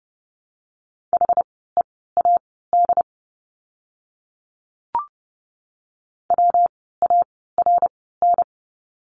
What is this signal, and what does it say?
Morse code, heard as a keyed tone, about 30 wpm, spaced as for 17 wpm, 700 Hz. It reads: HEAD WARN
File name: course / 30:10wpm / 115 Lesson 19f - Letter w - Words 30wpm